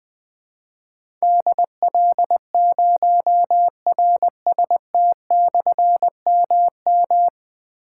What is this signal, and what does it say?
Morse code, heard as a keyed tone, about 21 wpm, 700 Hz. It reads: DL0RST/MM
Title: Difficult Call Signs 20wpm